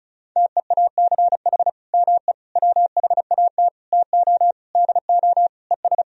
Morse code, heard as a keyed tone, about 36 wpm, 700 Hz. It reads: TEACH ME WHAT TO DO ES
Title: Sentences from Top 500 Words and Abbreviations 35wpm